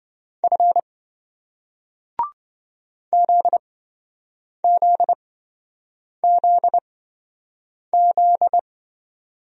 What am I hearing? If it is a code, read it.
Morse code, heard as a keyed tone, about 28 wpm, spaced as for 7 wpm, 700 Hz. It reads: F ZZZZ